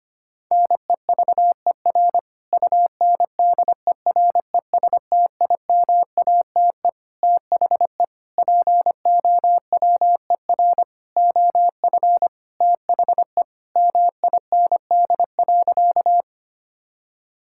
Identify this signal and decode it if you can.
Morse code, heard as a keyed tone, about 25 wpm, 700 Hz. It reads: NEVER UNDERESTIMATE THE POWER OF THE MIND.
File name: practice / 25wpm / Sentences with 4-Word Phrases - 25wpm